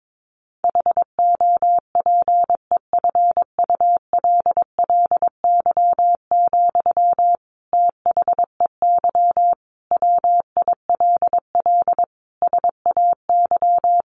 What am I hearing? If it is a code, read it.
Morse code, heard as a keyed tone, about 22 wpm, 700 Hz. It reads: HOPEFULLY, THEY WILL SAY
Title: Sentences from Top 1500 Words 22wpm